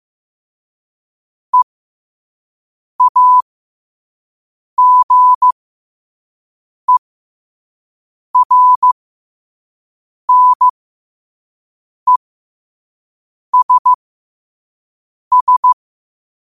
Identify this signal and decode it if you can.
Morse code, heard as a keyed tone, about 15 wpm, spaced as for 5 wpm, 1000 Hz. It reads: EAGERNESS
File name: qotdfiles / 5wpm